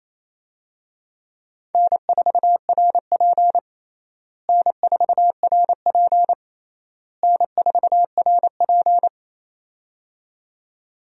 Morse code, heard as a keyed tone, about 28 wpm, 700 Hz. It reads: N4RP N4RP N4RP
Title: USA Call Signs - 4 Characters - Repeated 3x - 28wpm